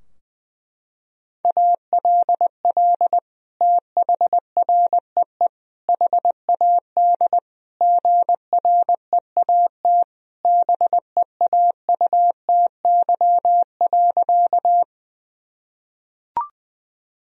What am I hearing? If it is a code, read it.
Morse code, heard as a keyed tone, about 21 wpm, 700 Hz. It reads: ALL THREE HAD GREAT BEAUTY.
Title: Sentences from Top 1000 Words - Review Part 2 - 20wpm